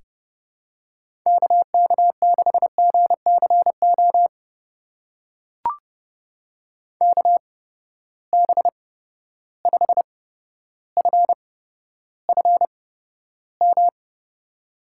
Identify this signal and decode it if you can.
Morse code, heard as a keyed tone, about 30 wpm, spaced as for 13 wpm, 700 Hz. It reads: KK6GCO KB5FFM
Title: USA Call Signs - 6 Characters - 8x Character Spacing - 30wpm